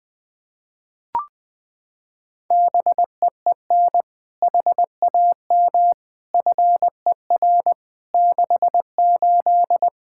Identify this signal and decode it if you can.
Morse code, heard as a keyed tone, about 21 wpm, 700 Hz. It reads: BEEN HAM FER 68
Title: Practice QSO - Exchange 3 - More Info - 20wpm